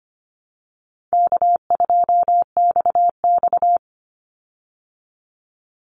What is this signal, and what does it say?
Morse code, heard as a keyed tone, about 25 wpm, 700 Hz. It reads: K2XX
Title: Easy Call Signs Repeated 3x - 25wpm